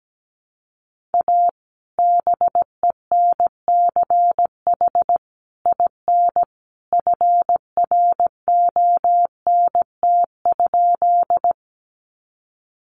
Morse code, heard as a keyed tone, about 17 wpm, 700 Hz. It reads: A BENCH IN FRONT?
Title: Sentences from Top 1400 Words 17wpm